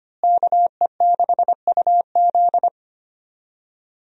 Morse code, heard as a keyed tone, about 25 wpm, 700 Hz. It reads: KE6UZ
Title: USA Call Signs - 5 Characters - Repeated 3x - 25wpm